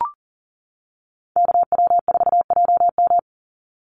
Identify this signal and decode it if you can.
Morse code, heard as a keyed tone, about 40 wpm, 700 Hz. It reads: KW4JM
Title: NA SOTA Activator Callsigns 40wpm